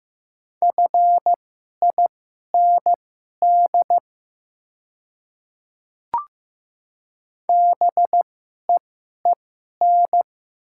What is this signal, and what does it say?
Morse code, heard as a keyed tone, about 15 wpm, spaced as for 11 wpm, 700 Hz. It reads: FIND BEEN
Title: 2x Character Spacing - Top 200 Words - 15wpm